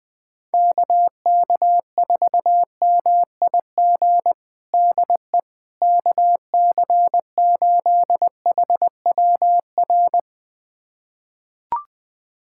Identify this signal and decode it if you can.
Morse code, heard as a keyed tone, about 19 wpm, 700 Hz. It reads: KK4MIG DE KC8HWR
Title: USA Callsign DE Callsign 20wpm